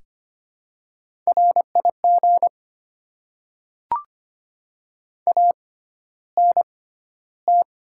Morse code, heard as a keyed tone, about 25 wpm, spaced as for 13 wpm, 700 Hz. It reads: RIG ANT
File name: Top 20 QSO Elements - 6x Character Spacing - 25wpm